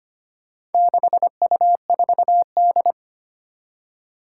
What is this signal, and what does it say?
Morse code, heard as a keyed tone, about 25 wpm, 700 Hz. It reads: 6U4D
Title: Four Challenging Letter-Number Combinations 25wpm